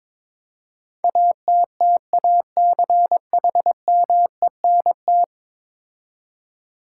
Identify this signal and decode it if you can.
Morse code, heard as a keyed tone, about 22 wpm, 700 Hz. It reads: ATTACHMENT
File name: practice / 22wpm / 10 Letter Words - First 3 Letters Given - 22wpm